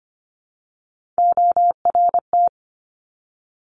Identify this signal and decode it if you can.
Morse code, heard as a keyed tone, about 25 wpm, 700 Hz. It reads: ORT
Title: Top Three Letter Combinations in Words 25wpm